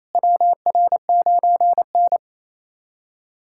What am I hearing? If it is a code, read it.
Morse code, heard as a keyed tone, about 28 wpm, 700 Hz. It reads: WR9N
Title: USA Callsign DE Callsign - 4 Characters - 28wpm